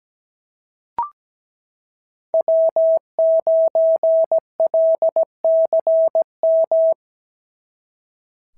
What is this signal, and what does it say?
Morse code, heard as a keyed tone, about 16 wpm, 650 Hz. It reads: W9LCM
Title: SOTA QSO Practice 17wpm